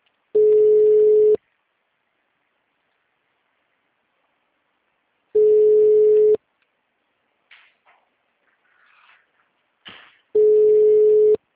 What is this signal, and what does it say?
Ringback tone (Europe, 425 Hz).
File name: out- -2042- - - .5049